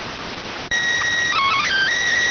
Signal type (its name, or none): SSTV (slow-scan television)